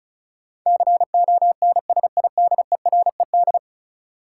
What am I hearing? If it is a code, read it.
Morse code, heard as a keyed tone, about 36 wpm, 700 Hz. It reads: CONSIDERED
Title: Sets of 5 Words - New Word Drill - 35wpm